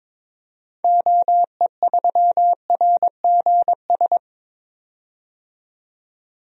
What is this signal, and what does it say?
Morse code, heard as a keyed tone, about 22 wpm, 700 Hz. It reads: OE3RGS
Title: DXCC-100 Call Signs - 6 Characters - 22wpm